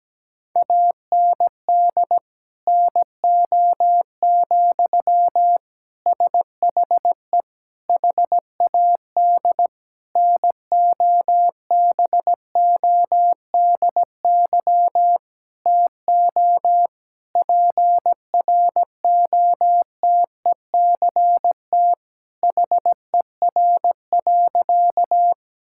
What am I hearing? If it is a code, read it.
Morse code, heard as a keyed tone, about 17 wpm, 700 Hz. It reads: AND NO, SHE HAD NOBODY TO PROTECT HER.
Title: Sentences from Top 1600 Words 17wpm